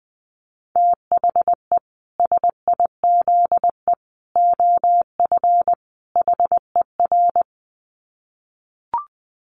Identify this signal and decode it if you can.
Morse code, heard as a keyed tone, about 20 wpm, 700 Hz. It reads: THE SIZE OF HER
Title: Dit Challenge 20wpm